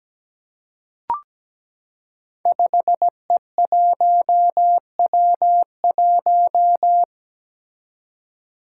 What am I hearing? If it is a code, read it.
Morse code, heard as a keyed tone, about 17 wpm, 700 Hz. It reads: 5E1W1
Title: Five Challenging Letter-Number Combinations 17wpm